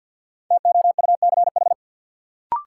Morse code, heard as a keyed tone, about 48 wpm, 700 Hz. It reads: TOUCH
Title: Sentences from Top 1200 Words - New Word Drill Alt - 50wpm